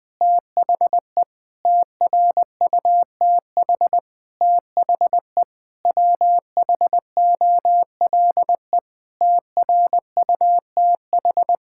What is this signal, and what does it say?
Morse code, heard as a keyed tone, about 21 wpm, 700 Hz. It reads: THE TRUTH THE WHOLE TRUTH
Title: Sentences with 9-Word Phrases - 20wpm